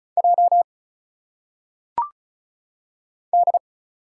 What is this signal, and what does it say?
Morse code, heard as a keyed tone, about 36 wpm, 700 Hz. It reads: J D
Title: Single Letters - Phonetic Pronunciation 35wpm